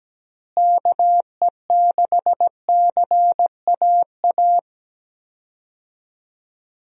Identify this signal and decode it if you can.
Morse code, heard as a keyed tone, about 17 wpm, 700 Hz. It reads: KE6CAA
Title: USA Call Signs - 6 Characters - Repeated 3x - 17wpm